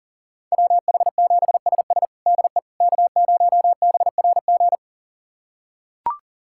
Morse code, heard as a keyed tone, about 40 wpm, 700 Hz. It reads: WH7SS DE K0BRG
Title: USA Callsign DE Callsign - 5 Characters - 40wpm